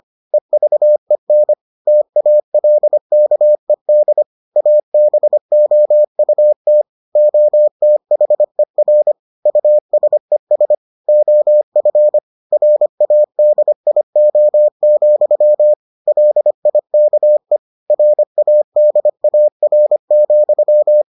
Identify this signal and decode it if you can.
Morse code, heard as a keyed tone, about 25 wpm, 600 Hz. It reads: EVEN TALKED ABOUT OTHER USES OF RADIO, LIKE RADAR,